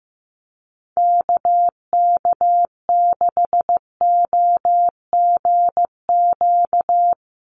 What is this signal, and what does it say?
Morse code, heard as a keyed tone, about 15 wpm, 700 Hz. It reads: KK6OGQ
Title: USA Call Signs 15wpm